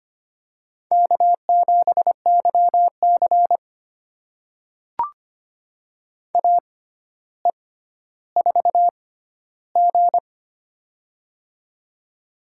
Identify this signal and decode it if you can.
Morse code, heard as a keyed tone, about 25 wpm, spaced as for 13 wpm, 700 Hz. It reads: K7YC AE4G
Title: USA Call Signs - 4 Characters - 6x Character Spacing - 25wpm